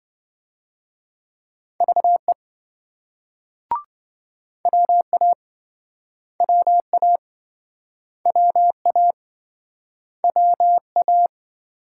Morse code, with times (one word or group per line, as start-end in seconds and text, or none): 1.8-2.3 VE
4.7-5.3 WA
6.4-7.2 WA
8.3-9.1 WA
10.2-11.3 WA